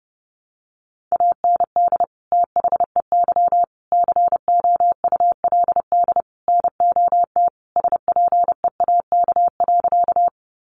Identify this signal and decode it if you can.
Morse code, heard as a keyed tone, about 30 wpm, 700 Hz. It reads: AND THEY COULD NOT SPEAK.
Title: Sentences from Top 1100 Words 30wpm